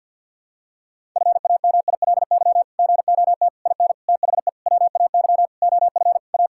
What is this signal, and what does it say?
Morse code, heard as a keyed tone, about 50 wpm, 700 Hz. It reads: FAMILY GOT IN THE WAY OF A
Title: Sentences from Top 1000 Words - Review Part 6 - 50wpm